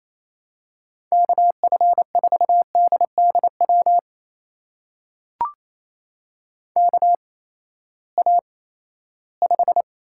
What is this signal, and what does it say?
Morse code, heard as a keyed tone, about 28 wpm, spaced as for 16 wpm, 700 Hz. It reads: KF4DDW KA5
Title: USA Call Signs - 6 Characters - 8x Character Spacing - 28wpm